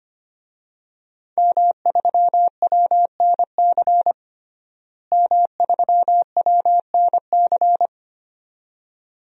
Morse code, heard as a keyed tone, about 25 wpm, 700 Hz. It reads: M3WNC M3WNC
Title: DXCC-100 Call Signs - 5 Characters - Repeated 2x - 25wpm